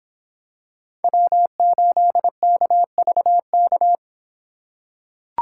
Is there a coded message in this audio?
Yes: Morse code from 1.0 to 4.0 s, heard as a keyed tone, about 25 wpm, 700 Hz. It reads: W8KVK